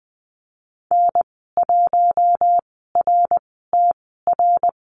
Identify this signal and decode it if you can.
Morse code, heard as a keyed tone, about 20 wpm, spaced as for 15 wpm, 700 Hz. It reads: N1RTR
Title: 062 Lesson 11g - Letter h - Call Signs 20wpm